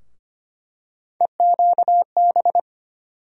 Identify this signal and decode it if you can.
Morse code, heard as a keyed tone, about 25 wpm, 700 Hz. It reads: EQB
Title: USA Call Sign Postfixes - 3 Letters 25wpm